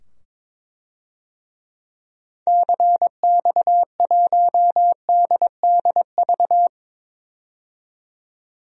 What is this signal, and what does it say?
Morse code, heard as a keyed tone, about 22 wpm, 700 Hz. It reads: CX1DDV
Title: DXCC-100 Call Signs - 6 Characters - 22wpm